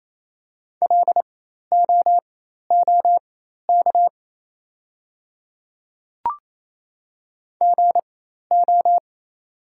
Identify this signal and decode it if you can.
Morse code, heard as a keyed tone, about 28 wpm, spaced as for 13 wpm, 690 Hz. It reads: LOOK GO
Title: 4x Character Spacing - Top 200 Words - 28wpm